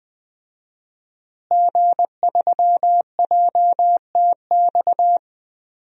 Morse code, heard as a keyed tone, about 20 wpm, 700 Hz. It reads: G3JTX
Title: DXCC-100 Call Signs - 5 Characters - Repeated 3x - 20wpm